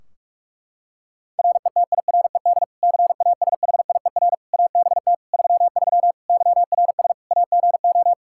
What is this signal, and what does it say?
Morse code, heard as a keyed tone, about 46 wpm, 700 Hz. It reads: RETIRED CASHIER ABT 33 YRS AGO
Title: Practice QSO - Exchange 3 - More Info - 45wpm